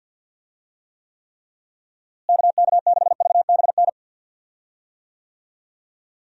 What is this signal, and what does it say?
Morse code, heard as a keyed tone, about 53 wpm, 700 Hz. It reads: KK6VBN
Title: USA Call Signs - 6 Characters - Repeated 3x - 50wpm